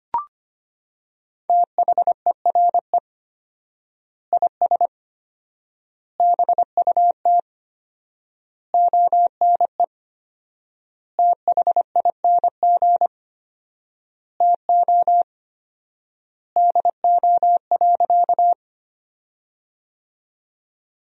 Morse code, heard as a keyed tone, about 25 wpm, 700 Hz. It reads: THERE IS BUT ONE THING TO DO.
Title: 4x Word-Spacing - Sentences from Top 100 Words - 25wpm